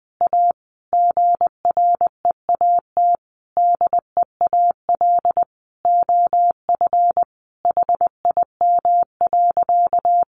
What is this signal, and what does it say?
Morse code, heard as a keyed tone, about 20 wpm, 700 Hz. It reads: A GREAT DEAL OF HIM.
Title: Sentences from Top 800 Words 20wpm